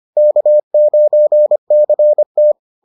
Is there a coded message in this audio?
Yes: Morse code from 0.2 to 2.5 s, heard as a keyed tone, about 25 wpm, 600 Hz. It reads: K9CT